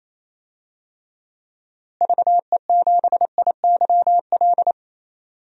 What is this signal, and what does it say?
Morse code, heard as a keyed tone, about 28 wpm, 700 Hz. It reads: VE7IYL
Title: DXCC-100 Call Signs - 6 Characters - Repeated 3x - 28wpm